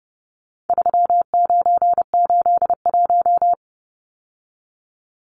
Morse code, heard as a keyed tone, about 30 wpm, 700 Hz. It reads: 3981
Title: Numbers - 4-Digit 30wpm